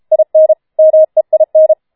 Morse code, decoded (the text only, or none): IN MEIN